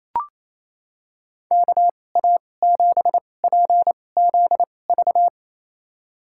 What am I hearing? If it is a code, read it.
Morse code, heard as a keyed tone, about 28 wpm, spaced as for 20 wpm, 690 Hz. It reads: KA7PZV
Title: USA Call Signs - 6 Characters - 2x Character Spacing - 28wpm